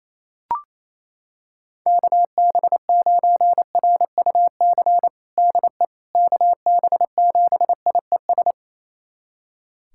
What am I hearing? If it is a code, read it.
Morse code, heard as a keyed tone, about 28 wpm, 700 Hz. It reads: KB9RUC DE KB7IES
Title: USA Callsign DE Callsign - 6 Characters - 28wpm